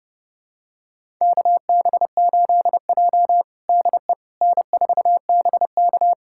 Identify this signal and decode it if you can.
Morse code, heard as a keyed tone, about 32 wpm, 700 Hz. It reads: KB8J DE N4BK
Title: USA Callsign DE Callsign - 4 Characters - 30wpm